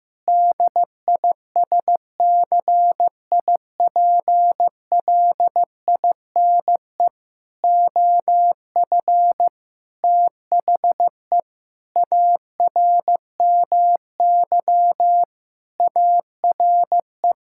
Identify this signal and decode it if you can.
Morse code, heard as a keyed tone, about 15 wpm, 700 Hz. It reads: DISCIPLINE OF THE ARMY ARE
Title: Sentences from Top 1000 Words - Review Part 6 - 15wpm